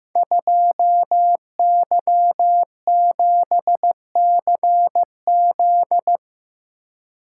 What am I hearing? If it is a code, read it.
Morse code, heard as a keyed tone, about 15 wpm, 700 Hz. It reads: 2Y7CZ
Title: Code Groups - 5 Letter:Numbers - 15wpm